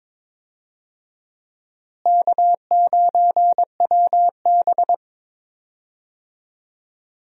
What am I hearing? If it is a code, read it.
Morse code, heard as a keyed tone, about 21 wpm, 700 Hz. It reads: K9WB
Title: USA Call Signs - 4 Characters - Repeated 2x - 22wpm